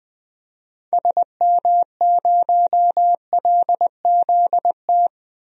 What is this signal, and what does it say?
Morse code, heard as a keyed tone, about 20 wpm, 700 Hz. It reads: SM0LZT
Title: DXCC-100 Call Signs - 6 Characters - 20wpm